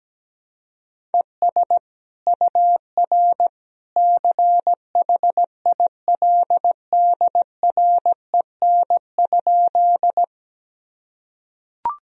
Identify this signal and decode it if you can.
Morse code, heard as a keyed tone, about 17 wpm, 700 Hz. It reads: ES UR CHILDREN?
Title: Sentences from Top 300 Words and Abbreviations 17wpm